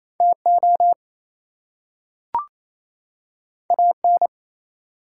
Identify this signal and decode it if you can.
Morse code, heard as a keyed tone, about 27 wpm, 700 Hz. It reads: TO AN
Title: Two Letter Words - 28wpm